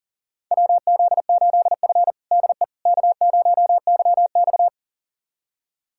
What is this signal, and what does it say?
Morse code, heard as a keyed tone, about 40 wpm, 700 Hz. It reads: WZ8F DE K0YX
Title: USA Callsign DE Callsign - 4 Characters - 40wpm